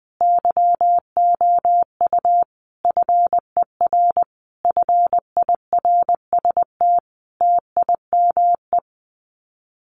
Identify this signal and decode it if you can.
Morse code, heard as a keyed tone, about 20 wpm, 700 Hz. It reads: YOU FER FIRST TIME
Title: Common QSO Phrases 20wpm